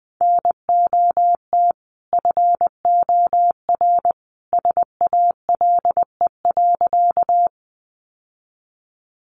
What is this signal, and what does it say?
Morse code, heard as a keyed tone, about 20 wpm, 700 Hz. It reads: NOT FOR SALE.
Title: Sentences from Top 1400 Words 20wpm